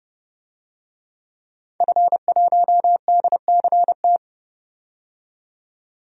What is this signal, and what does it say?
Morse code, heard as a keyed tone, about 30 wpm, 700 Hz. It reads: F1DCT
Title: DXCC-100 Call Signs - 5 Characters - 30wpm